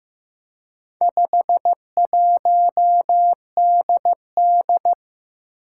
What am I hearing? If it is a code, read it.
Morse code, heard as a keyed tone, about 15 wpm, 700 Hz. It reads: 51DD
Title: Four Challenging Letter-Number Combinations 15wpm